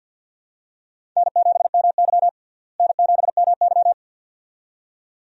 Morse code, heard as a keyed tone, about 49 wpm, 700 Hz. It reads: N7MY N7MY
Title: USA Call Signs - 4 Characters - Repeated 2x - 50wpm